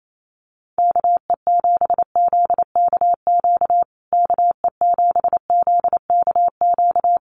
Morse code, heard as a keyed tone, about 28 wpm, 700 Hz. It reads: KE7ZKQ KE7ZKQ
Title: USA Call Signs Repeated 2x 28wpm